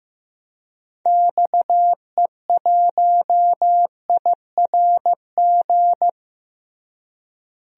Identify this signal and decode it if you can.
Morse code, heard as a keyed tone, about 16 wpm, 700 Hz. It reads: XE1IRG
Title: DXCC-100 Call Signs - 6 Characters - Repeated 2x - 15wpm